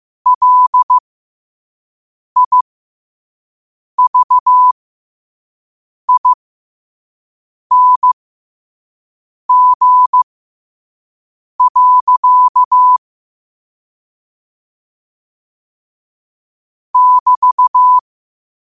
Morse code, heard as a keyed tone, about 15 wpm, spaced as for 5 wpm, 1000 Hz. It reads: LIVING. <BT>
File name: qotdfiles / 5wpm